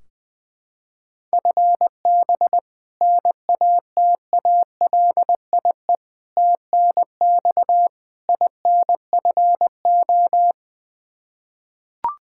Morse code, heard as a keyed tone, about 20 wpm, 700 Hz. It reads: FB NATALIE TNX INFO
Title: Practice QSO - Exchange 3 - More Info - 20wpm